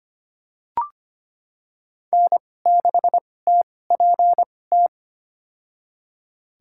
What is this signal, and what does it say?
Morse code, heard as a keyed tone, about 25 wpm, spaced as for 18 wpm, 700 Hz. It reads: N6TPT
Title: USA Call Signs - 5 Characters - 2x Character Spacing - 25wpm